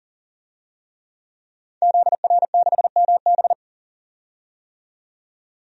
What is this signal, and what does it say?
Morse code, heard as a keyed tone, about 40 wpm, 700 Hz. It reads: ZR6MB
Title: DXCC-100 Call Signs - 5 Characters - 40wpm